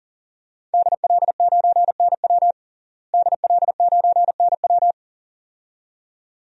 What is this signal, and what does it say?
Morse code, heard as a keyed tone, about 37 wpm, 700 Hz. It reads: DL9NW DL9NW
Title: DXCC-100 Call Signs - 5 Characters - Repeated 2x - 40wpm